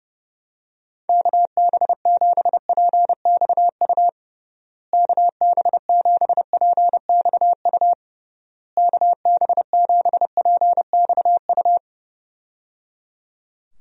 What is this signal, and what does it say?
Morse code, heard as a keyed tone, about 30 wpm, 700 Hz. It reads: KB7PXU KB7PXU KB7PXU
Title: USA Call Signs - 6 Characters - Repeated 3x - 30wpm